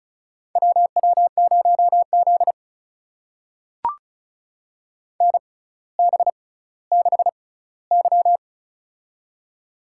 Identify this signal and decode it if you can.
Morse code, heard as a keyed tone, about 36 wpm, spaced as for 18 wpm, 700 Hz. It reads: WW0Z NB6Y
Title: USA Call Signs - 4 Characters - 6x Character Spacing - 35wpm